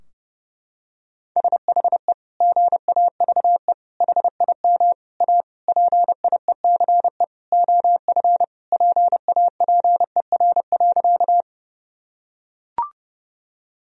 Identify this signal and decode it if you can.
Morse code, heard as a keyed tone, about 30 wpm, 700 Hz. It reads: SHE GAVE HIM A PIECE OF PAPER.
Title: Sentences from Top 1000 Words - Review Part 2 - 30wpm